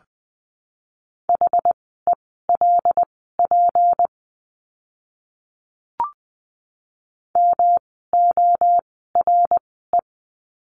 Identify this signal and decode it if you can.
Morse code, heard as a keyed tone, about 20 wpm, spaced as for 15 wpm, 700 Hz. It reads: HELP MORE
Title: 206 Lesson 33f - Letter j - Words 20wpm